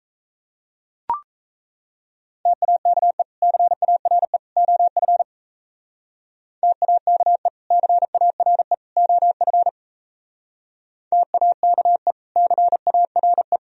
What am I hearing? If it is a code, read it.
Morse code, heard as a keyed tone, about 35 wpm, 700 Hz. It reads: TAKE CARE OF TAKE CARE OF TAKE CARE
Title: Speed-Racing - Sets of 3 Words 28wpm